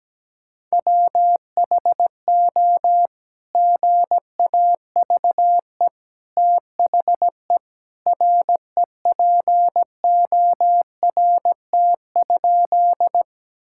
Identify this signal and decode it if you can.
Morse code, heard as a keyed tone, about 17 wpm, 700 Hz. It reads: WHO GAVE THE REPORT?
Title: Sentences from Top 1000 Words - Review Part 6 - 17wpm